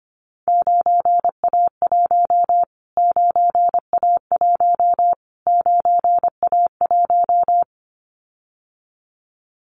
Morse code, heard as a keyed tone, about 25 wpm, 700 Hz. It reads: 9A1 9A1 9A1
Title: Callsign Prefixes for Top 25 DXCC - Repeated 3x - 25wpm